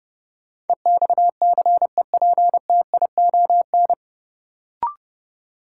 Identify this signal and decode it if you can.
Morse code, heard as a keyed tone, about 30 wpm, 700 Hz. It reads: EXCEPTION
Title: Sets of 4 Words - New Word Drill - 30wpm